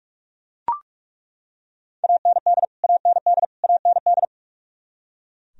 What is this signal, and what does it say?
Morse code, heard as a keyed tone, about 41 wpm, 700 Hz. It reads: AND AND AND
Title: Top 100 Words Repeated 45wpm